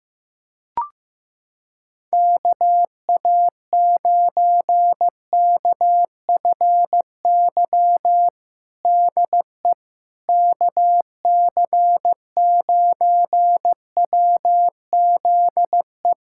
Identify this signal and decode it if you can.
Morse code, heard as a keyed tone, about 15 wpm, 700 Hz. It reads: KA9KFY DE KC9WZE
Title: USA Callsign DE Callsign - 6 Characters - 15wpm